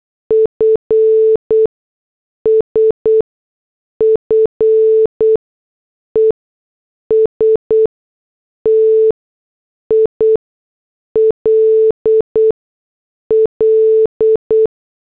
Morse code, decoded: FSFE STILL